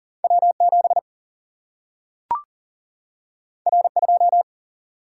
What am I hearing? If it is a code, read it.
Morse code, heard as a keyed tone, about 40 wpm, 700 Hz. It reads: W7 R2
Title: Single-Letter Single-Number 40wpm